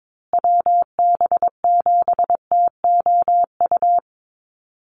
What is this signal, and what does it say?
Morse code, heard as a keyed tone, about 22 wpm, 700 Hz. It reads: WB7TOU
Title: USA Call Signs Repeated 2x 22wpm